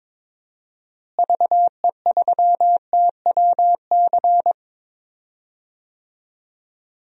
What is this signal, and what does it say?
Morse code, heard as a keyed tone, about 22 wpm, 700 Hz. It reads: VE3TWC
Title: DXCC-100 Call Signs - 6 Characters - Repeated 3x - 22wpm